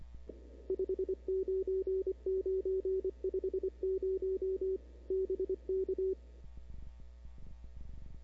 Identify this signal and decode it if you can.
Morse code, heard as a keyed tone, about 23 wpm, 340 Hz. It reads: 599 50 BK
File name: CW / 2005-k5tr arrl cw dx -00:00